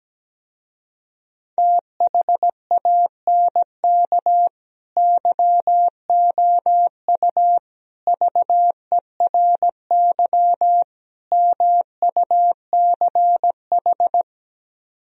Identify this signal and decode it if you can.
Morse code, heard as a keyed tone, about 17 wpm, 700 Hz. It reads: THANK YOU VERY MUCH
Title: Speed-Racing - Sets of 4 Words 17wpm